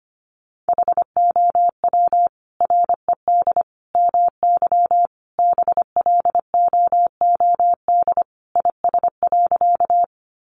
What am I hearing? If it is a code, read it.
Morse code, heard as a keyed tone, about 25 wpm, 690 Hz. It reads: HOW RED MY BLOOD IS.
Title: Sentences from Top 1100 Words 25wpm